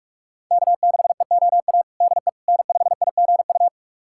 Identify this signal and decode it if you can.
Morse code, heard as a keyed tone, about 44 wpm, 700 Hz. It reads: K6EOA DE N5IGU
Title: USA Callsign DE Callsign - 5 Characters - 45wpm